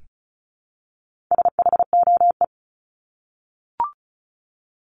Morse code, heard as a keyed tone, about 34 wpm, 700 Hz. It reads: SHOE
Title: Top 1751-2000 Words 35wpm